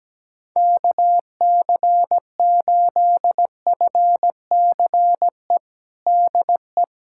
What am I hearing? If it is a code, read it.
Morse code, heard as a keyed tone, about 17 wpm, 700 Hz. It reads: KC8FCE DE
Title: USA Callsign DE Callsign - 6 Characters - 17wpm